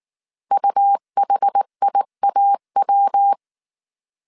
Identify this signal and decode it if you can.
Morse code, heard as a keyed tone, about 19 wpm, 800 Hz. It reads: UHIAW